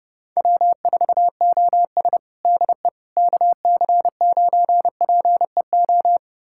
Morse code, heard as a keyed tone, about 30 wpm, 700 Hz. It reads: W4OS DE KC9PEO
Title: USA Callsign DE Callsign 30wpm